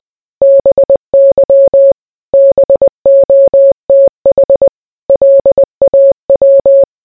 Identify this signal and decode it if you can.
Morse code, heard as a keyed tone, about 20 wpm, 550 Hz. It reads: BY BOTH LAW